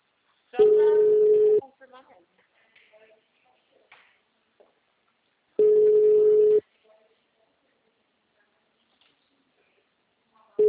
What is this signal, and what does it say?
Ringback tone (Europe, 425 Hz), with speech.